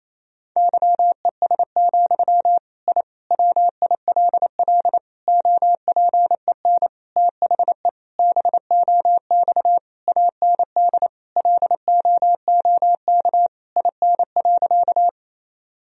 Morse code, heard as a keyed tone, about 28 wpm, 700 Hz. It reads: YES, I WILL OPEN THE BOX AND LOOK IN.
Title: Sentences from Top 1000 Words - Review Part 2 - 28wpm